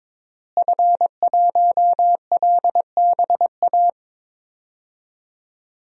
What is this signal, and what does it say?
Morse code, heard as a keyed tone, about 22 wpm, 700 Hz. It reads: F1LBA
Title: DXCC-100 Call Signs - 5 Characters - Repeated 2x - 22wpm